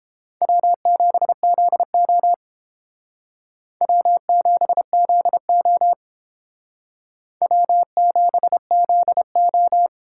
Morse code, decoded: W7ZO W7ZO W7ZO

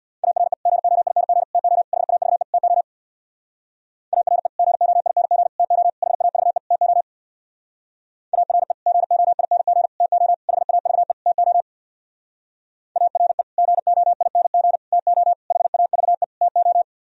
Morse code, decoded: ARE GOING TO HAVE TO ARE GOING TO HAVE TO ARE GOING TO HAVE TO ARE GOING TO HAVE TO

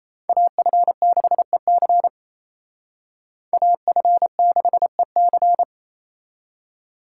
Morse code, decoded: AF6EC AF6EC

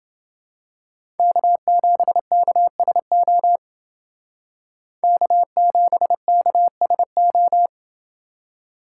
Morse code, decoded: K7KSO K7KSO

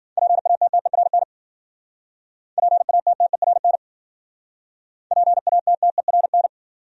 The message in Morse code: PATTERN PATTERN PATTERN

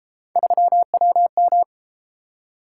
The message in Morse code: 3WM

